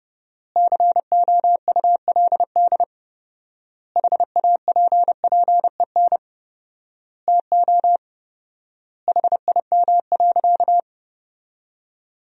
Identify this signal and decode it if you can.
Morse code, heard as a keyed tone, about 32 wpm, 700 Hz. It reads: COULD HAPPEN TO HIM.
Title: 4x Word-Spacing - Sentences from Top 400 Words - 30wpm